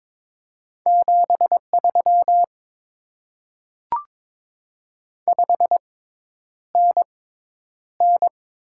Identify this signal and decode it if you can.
Morse code, heard as a keyed tone, about 22 wpm, spaced as for 10 wpm, 690 Hz. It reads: 73 5NN
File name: Top 20 QSO Elements - 6x Character Spacing - 22wpm